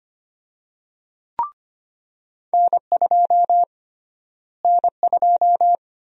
Morse code, heard as a keyed tone, about 25 wpm, 700 Hz. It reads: N2 N2
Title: USA Call Sign Prefixes - Repeated 2x - 25wpm